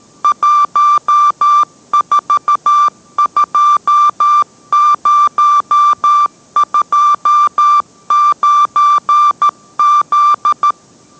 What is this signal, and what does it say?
Morse code, heard as a keyed tone, about 13 wpm, 1180 Hz. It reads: 142029Z